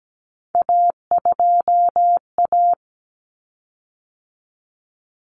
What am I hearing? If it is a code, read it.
Morse code, heard as a keyed tone, about 17 wpm, 690 Hz. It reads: A2A
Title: Common Cut Numbers - 3-Digit 17wpm